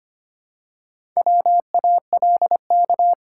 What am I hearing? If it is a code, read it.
Morse code, heard as a keyed tone, about 25 wpm, 700 Hz. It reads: WALK
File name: practice / 25wpm / Sentences with 2-Word Phrases - 25wpm